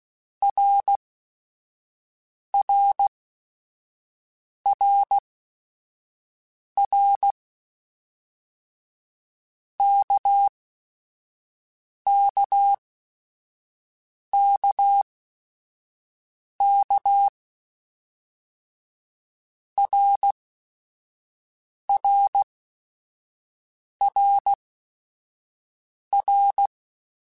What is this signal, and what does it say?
Morse code, heard as a keyed tone, about 16 wpm, spaced as for 5 wpm, 800 Hz. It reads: RRRR KKKK RRRR
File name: Basic Sets / Basic Letter Groups Collective7 / RKPX@5